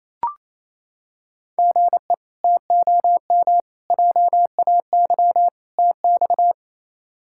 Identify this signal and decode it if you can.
Morse code, heard as a keyed tone, about 28 wpm, 700 Hz. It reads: GE TOM JAY TX
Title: K1USN SST QSO Practice - 28wpm